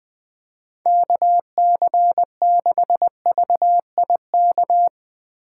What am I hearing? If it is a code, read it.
Morse code, heard as a keyed tone, about 21 wpm, 700 Hz. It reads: KC6VIK